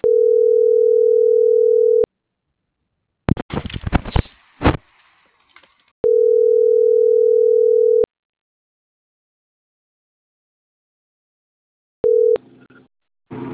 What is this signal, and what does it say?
Ringback tone (North America), 440 + 480 Hz.